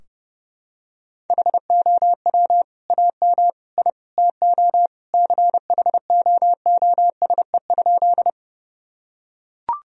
Morse code, heard as a keyed tone, about 29 wpm, 700 Hz. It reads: HOW AM I TO CHOOSE?
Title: Sentences from Top 1000 Words - Review Part 4 - 30wpm